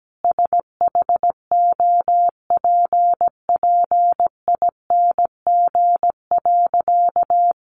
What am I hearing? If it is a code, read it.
Morse code, heard as a keyed tone, about 17 wpm, 700 Hz. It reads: SHOPPING.
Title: Additional Easy Phrases 17wpm